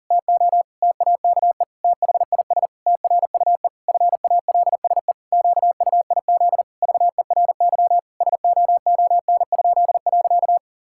Morse code, heard as a keyed tone, about 37 wpm, 700 Hz. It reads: TO TAKE THIS TRUE FALSE QUIZ VERY SOON?.
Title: Pangram Challenge v2 - 40wpm